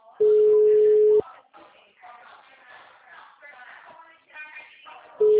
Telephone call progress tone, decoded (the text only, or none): ringback tone